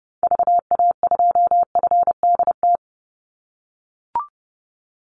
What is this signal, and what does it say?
Morse code, heard as a keyed tone, about 30 wpm, 700 Hz. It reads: VA2FDT
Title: Call Signs - 30wpm